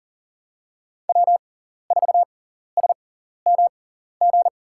Morse code, heard as a keyed tone, about 40 wpm, spaced as for 14 wpm, 700 Hz. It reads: W4SMG
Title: USA Call Signs - 5 Characters - 6x Character Spacing - 40wpm